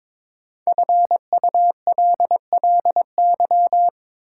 Morse code, heard as a keyed tone, about 22 wpm, 700 Hz. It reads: FULLY
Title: Sentences from Top 1300 Words - New Word Drill Alt - 22wpm